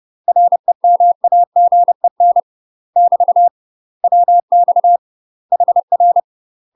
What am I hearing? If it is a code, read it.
Morse code, heard as a keyed tone, about 30 wpm, 700 Hz. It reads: REMAGEN <BT> WX HR
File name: UK MASTER QSOs TRAINING / 30WPM / UK MASTER QSOs 0045